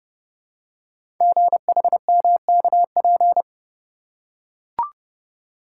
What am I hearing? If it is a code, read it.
Morse code, heard as a keyed tone, about 28 wpm, 700 Hz. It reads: GHMKP
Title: Code Groups - 5 Letters - 30wpm